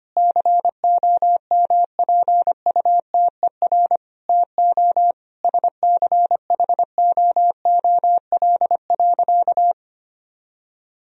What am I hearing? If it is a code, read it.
Morse code, heard as a keyed tone, about 25 wpm, 700 Hz. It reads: COMPUTER TO SCHOOL.
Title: Sentences from Top 1000 Words - Review Part 3 - 25wpm